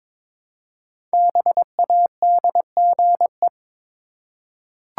Morse code, heard as a keyed tone, about 22 wpm, 690 Hz. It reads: BADGE
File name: 5 Letter Words - First 3 Letters Given - 22wpm